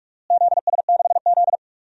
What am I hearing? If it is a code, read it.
Morse code, heard as a keyed tone, about 46 wpm, 700 Hz. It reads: ZS6Z
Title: DXCC-100 Call Signs - 4 Characters - Repeated 2x - 45wpm